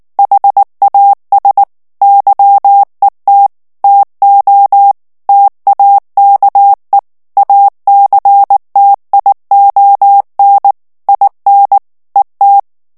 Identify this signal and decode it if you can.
Morse code, heard as a keyed tone, about 19 wpm, 800 Hz. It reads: HAS YET TO TAKE ACTION IN ET
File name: ANGLAIS texte clair 3 vitesse 18 mots minute